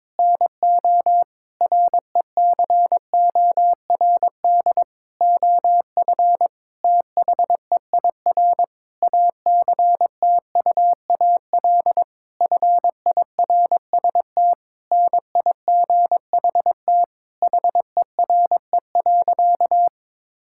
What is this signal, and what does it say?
Morse code, heard as a keyed tone, about 22 wpm, 700 Hz. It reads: NO RECORD OF THEIR ACTUAL FIRST NIGHT HERE.
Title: Sentences from Top 1000 Words - Review Part 5 - 22wpm